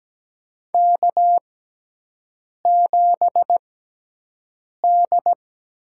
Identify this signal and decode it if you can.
Morse code, heard as a keyed tone, about 17 wpm, spaced as for 6 wpm, 700 Hz. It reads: K7D
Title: USA Call Signs - 5 Characters - 6x Character Spacing - 17wpm